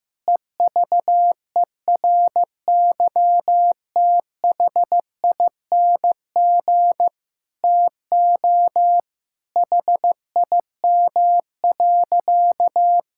Morse code, heard as a keyed tone, about 15 wpm, 700 Hz. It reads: EVERYTHING TO HIM.
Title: Sentences from Top 1600 Words 15wpm